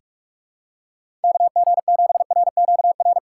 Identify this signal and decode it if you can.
Morse code, heard as a keyed tone, about 46 wpm, 700 Hz. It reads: KG7RQR
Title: USA Call Signs - 6x Character Spacing - 45wpm